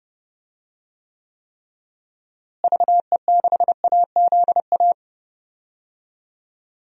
Morse code, heard as a keyed tone, about 30 wpm, 700 Hz. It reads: VE6AZA
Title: DXCC-100 Call Signs - 6 Characters - Repeated 3x - 30wpm